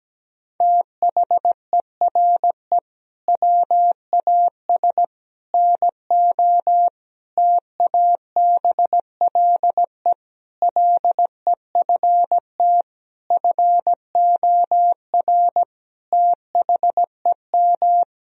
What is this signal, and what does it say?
Morse code, heard as a keyed tone, about 17 wpm, 700 Hz. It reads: THERE WAS NO TABLE LEFT FOR THEM
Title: Sentences from Top 1000 Words - Review Part 2 - 17wpm